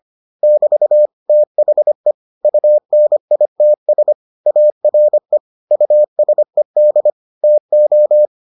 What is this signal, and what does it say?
Morse code, heard as a keyed tone, about 25 wpm, 600 Hz. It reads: <BT> THE UNITS ARE USED TO